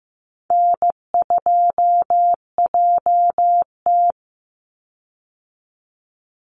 Morse code, heard as a keyed tone, about 15 wpm, 700 Hz. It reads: N2JT
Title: Easy Call Signs - 15wpm